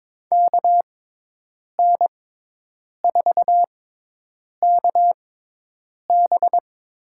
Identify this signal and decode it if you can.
Morse code, heard as a keyed tone, about 21 wpm, spaced as for 8 wpm, 700 Hz. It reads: KN4KB
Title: USA Call Signs - 6x Character Spacing - 22wpm